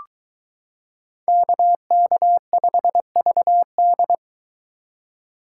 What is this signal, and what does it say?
Morse code, heard as a keyed tone, about 23 wpm, 700 Hz. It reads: KK5VD